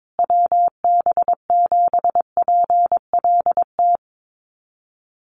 Morse code, heard as a keyed tone, about 22 wpm, 700 Hz. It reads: WB7PLT
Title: USA Call Signs Repeated 3x 22wpm